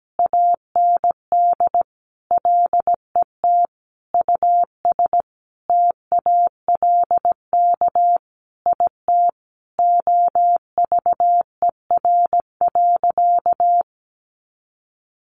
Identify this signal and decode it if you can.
Morse code, heard as a keyed tone, about 17 wpm, 700 Hz. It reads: AND LET US TALK IT OVER.
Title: Sentences from Top 500 Words 17wpm